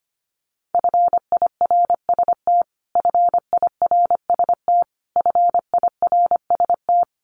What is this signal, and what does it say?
Morse code, heard as a keyed tone, about 25 wpm, 700 Hz. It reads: FIRST FIRST FIRST
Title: Top 200 Words Repeated 25wpm